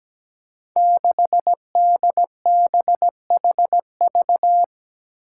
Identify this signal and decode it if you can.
Morse code, heard as a keyed tone, about 17 wpm, 700 Hz. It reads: 6DBHV
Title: Five Challenging Letter-Number Combinations 17wpm